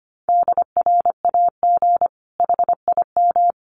Morse code, heard as a keyed tone, about 25 wpm, 700 Hz. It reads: DRAG HIM
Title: Sentences from Top 1500 Words 25wpm